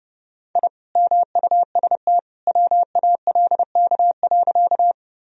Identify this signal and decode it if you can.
Morse code, heard as a keyed tone, about 30 wpm, 700 Hz. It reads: I MUST WALK.
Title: Sentences from Top 1000 Words - Review Part 1 - 30wpm